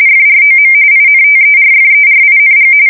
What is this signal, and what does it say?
RTTY, about 45.45 baud. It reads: FROM INTERFERING
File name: fsk170 / USS-New-Jersey